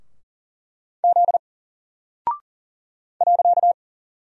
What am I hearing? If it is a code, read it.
Morse code, heard as a keyed tone, about 40 wpm, 700 Hz. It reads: Z .
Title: Single Character (Letter:Number:Punctuation:Prosign) 40wpm